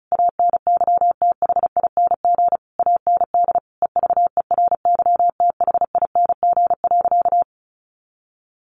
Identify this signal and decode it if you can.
Morse code, heard as a keyed tone, about 34 wpm, 700 Hz. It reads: ANYTHING AND EVERYTHING.
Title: Sentences from Top 1200 Words 35wpm